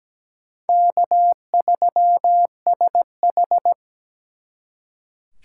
Morse code, heard as a keyed tone, about 20 wpm, 700 Hz. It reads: K3SH
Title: Speed Racing - USA 1x2 Call Signs 17wpm